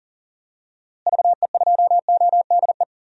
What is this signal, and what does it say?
Morse code, heard as a keyed tone, about 40 wpm, 700 Hz. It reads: VE2ODE
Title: DXCC-100 Call Signs - 6 Characters - Repeated 2x - 40wpm